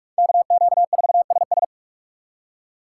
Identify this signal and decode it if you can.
Morse code, heard as a keyed tone, about 56 wpm, 700 Hz. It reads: KQ4SS
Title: Speed Racing - USA 2x2 Call Signs 45wpm